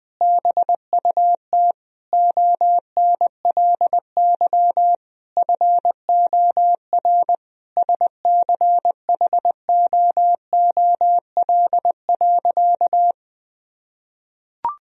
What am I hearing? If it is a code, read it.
Morse code, heard as a keyed tone, about 20 wpm, 700 Hz. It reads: BUT ONLY FOR SCHOOL.
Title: Sentences from Top 1000 Words - Review Part 1 - 20wpm